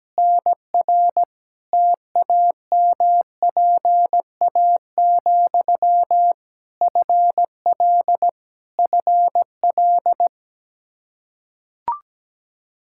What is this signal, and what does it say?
Morse code, heard as a keyed tone, about 17 wpm, 700 Hz. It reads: NR TAMPA, FL FL
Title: K1USN SST QSO Practice 17wpm